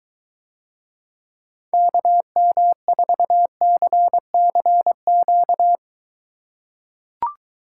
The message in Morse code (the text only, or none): KM4CCQ